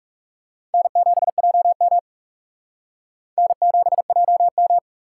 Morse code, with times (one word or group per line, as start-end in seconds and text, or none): 0.7-2.0 N7JM
3.4-4.8 N7JM